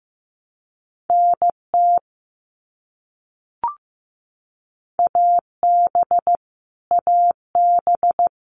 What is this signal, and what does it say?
Morse code, heard as a keyed tone, about 15 wpm, 700 Hz. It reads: NT AB AB
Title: Canadian Provinces and Territories - Repeated 3x - 15wpm